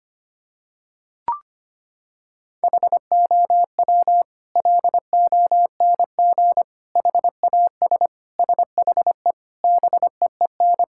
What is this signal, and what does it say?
Morse code, heard as a keyed tone, about 25 wpm, 700 Hz. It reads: HOW LONG HAS SHE BEEN
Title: Sentences from Top 1000 Words - Review Part 6 - 25wpm